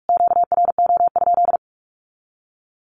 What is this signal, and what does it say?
Morse code, heard as a keyed tone, about 44 wpm, 700 Hz. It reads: QRO?
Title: Q-Codes Repeated 3x 45wpm